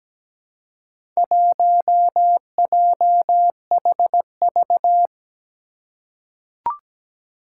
Morse code, heard as a keyed tone, about 17 wpm, 700 Hz. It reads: 1JHV
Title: Four Challenging Letter-Number Combinations 17wpm